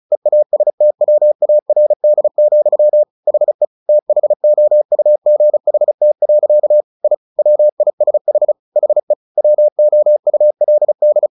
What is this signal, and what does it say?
Morse code, heard as a keyed tone, about 34 wpm, 600 Hz. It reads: EASTWARD, HE THOUGHT. I WISH HE WOULD
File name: the old man and the sea / the old man and the sea 1 35wpm